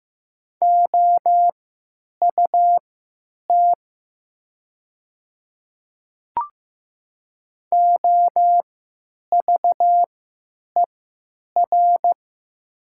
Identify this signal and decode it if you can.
Morse code, heard as a keyed tone, about 15 wpm, spaced as for 9 wpm, 700 Hz. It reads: OUT OVER
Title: 3x Character Spacing - Top 100 Words - 15wpm